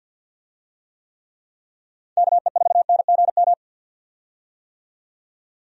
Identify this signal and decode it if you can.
Morse code, heard as a keyed tone, about 51 wpm, 700 Hz. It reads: KE4NGM
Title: USA Call Signs - 6 Characters - 50wpm